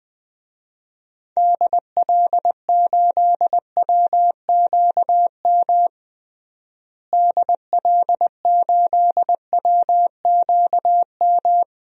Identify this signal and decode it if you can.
Morse code, heard as a keyed tone, about 21 wpm, 700 Hz. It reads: DL8WQM DL8WQM
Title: DXCC-100 Call Signs - 6 Characters - Repeated 2x - 20wpm